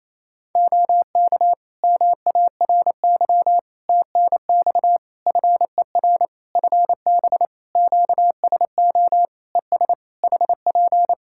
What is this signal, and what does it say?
Morse code, heard as a keyed tone, about 28 wpm, 700 Hz. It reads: OK MARY TNX FER FB QSO ES HP
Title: Practice QSO - Exchange 4 - Ending - 28wpm